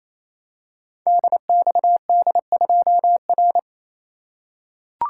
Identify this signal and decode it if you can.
Morse code, heard as a keyed tone, about 28 wpm, 700 Hz. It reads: DXD2R
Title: Code Groups - 5 Letter:Numbers - 28wpm